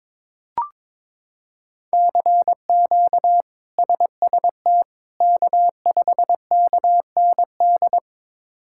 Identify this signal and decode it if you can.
Morse code, heard as a keyed tone, about 22 wpm, 700 Hz. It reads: CQ SST K5KND
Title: K1USN SST QSO Practice - 22wpm